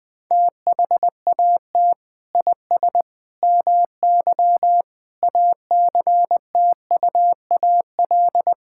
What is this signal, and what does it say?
Morse code, heard as a keyed tone, about 20 wpm, 700 Hz. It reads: THAT IS MY ACTUAL
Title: Sentences from Top 1000 Words - Review Part 5 - 20wpm